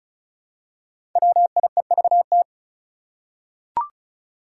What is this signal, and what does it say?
Morse code, heard as a keyed tone, about 36 wpm, 700 Hz. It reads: WIEVT
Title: Five Challenging Letter-Number Combinations 35wpm